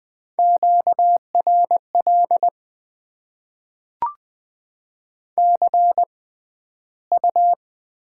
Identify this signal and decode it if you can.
Morse code, heard as a keyed tone, about 20 wpm, 700 Hz. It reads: Q R L C U
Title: Top 20 QSO Elements - 6x Character Spacing - 20wpm